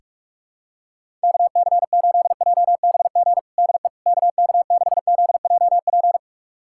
Morse code, heard as a keyed tone, about 46 wpm, 700 Hz. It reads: KC8JBG DE KK6ZJP
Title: USA Callsign DE Callsign - 6 Characters - 45wpm